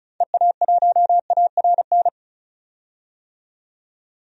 Morse code, heard as a keyed tone, about 36 wpm, 700 Hz. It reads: EA1ARN
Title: DXCC-100 Call Signs - 6 Characters - 35wpm